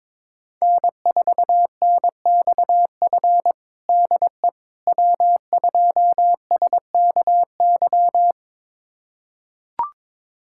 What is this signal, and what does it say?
Morse code, heard as a keyed tone, about 22 wpm, 700 Hz. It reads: N4NXF DE W2SKY
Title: USA Callsign DE Callsign - 5 Characters - 22wpm